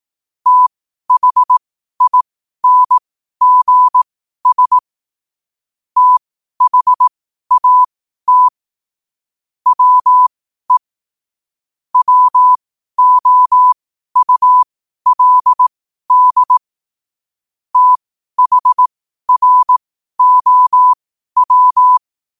Morse code, heard as a keyed tone, about 18 wpm, spaced as for 12 wpm, 1000 Hz. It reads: THINGS THAT WE WOULD THROW
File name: 12wpm